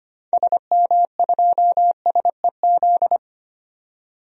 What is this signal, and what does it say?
Morse code, heard as a keyed tone, about 25 wpm, 700 Hz. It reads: SM2SEZ
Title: DXCC-100 Call Signs - 6 Characters - 25wpm